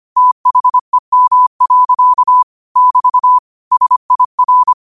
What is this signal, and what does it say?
Morse code, heard as a keyed tone, about 25 wpm, 1000 Hz. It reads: THEM. <BT> SIR
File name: qotdfiles / 25wpm